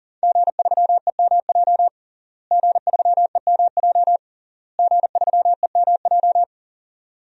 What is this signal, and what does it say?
Morse code, heard as a keyed tone, about 40 wpm, 700 Hz. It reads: G3EMJ G3EMJ G3EMJ
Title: DXCC-100 Call Signs - 5 Characters - Repeated 3x - 40wpm